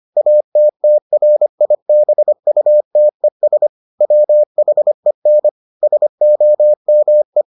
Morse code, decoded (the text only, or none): ATTRIBUTES WHEN SOME